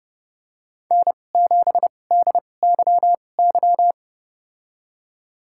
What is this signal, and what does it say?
Morse code, heard as a keyed tone, about 30 wpm, spaced as for 22 wpm, 700 Hz. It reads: N7DYY
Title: USA Call Signs - 5 Characters - 2x Character Spacing - 30wpm